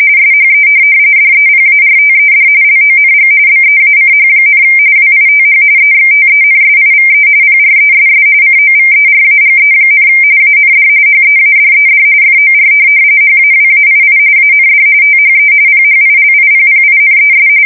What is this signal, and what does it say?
RTTY, about 45.45 baud. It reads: SHE DEPARTED LONG BEACH, WITH EXTENSIVE PRESS COVERAGE, AND HEADED FOR HAWAII AND THE PHILIPPINES
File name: fsk170 / USS-Missouri